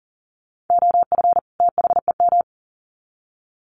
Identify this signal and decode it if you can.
Morse code, heard as a keyed tone, about 40 wpm, 700 Hz. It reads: OF THEM
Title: Sets of 2 Words 40wpm